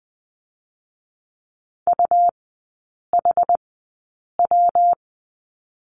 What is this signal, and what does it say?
Morse code, heard as a keyed tone, about 20 wpm, spaced as for 8 wpm, 700 Hz. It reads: UHW
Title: 138 Lesson 23e - Letter f - Three Characters 20wpm